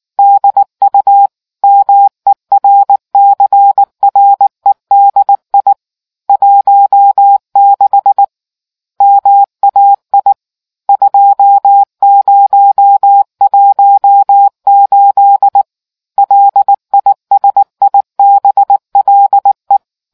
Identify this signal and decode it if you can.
Morse code, heard as a keyed tone, about 19 wpm, 800 Hz. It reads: DU MERCREDI 16 MAI 2018 LISIBLE